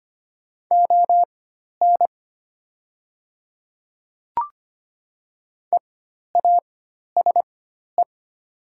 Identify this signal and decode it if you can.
Morse code, heard as a keyed tone, about 25 wpm, spaced as for 12 wpm, 700 Hz. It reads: ON EASE
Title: 4x Character Spacing - Top 300 Words - 25wpm